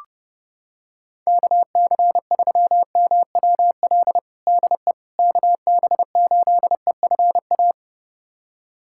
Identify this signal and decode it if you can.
Morse code, heard as a keyed tone, about 31 wpm, 700 Hz. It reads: KC3MWL DE KB8EFA
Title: USA Callsign DE Callsign - 6 Characters - 30wpm